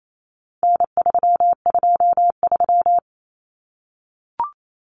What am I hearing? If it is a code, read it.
Morse code, heard as a keyed tone, about 28 wpm, 700 Hz. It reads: N323
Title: Common Cut Numbers - 4-Digit 28wpm